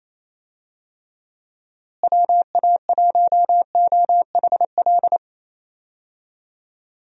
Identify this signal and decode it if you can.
Morse code, heard as a keyed tone, about 28 wpm, 700 Hz. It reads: WA1OHL
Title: USA Call Signs - 6 Characters - Repeated 2x - 28wpm